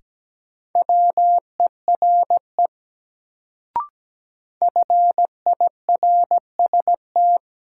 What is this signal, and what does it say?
Morse code, heard as a keyed tone, about 17 wpm, 700 Hz. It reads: WERE FIRST
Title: Sets of 2 Words - New Word Drill - 17wpm